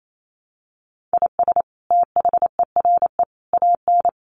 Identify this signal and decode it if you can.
Morse code, heard as a keyed tone, about 28 wpm, 700 Hz. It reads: IS THERE AN
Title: Sentences from Top 1300 Words 28wpm